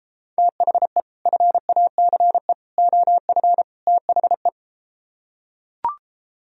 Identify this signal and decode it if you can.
Morse code, heard as a keyed tone, about 33 wpm, 700 Hz. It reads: THE FACE OF THE